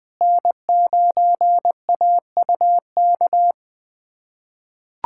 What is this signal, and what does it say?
Morse code, heard as a keyed tone, about 21 wpm, 700 Hz. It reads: N9AUK
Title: USA Callsign DE Callsign - 5 Characters - 20wpm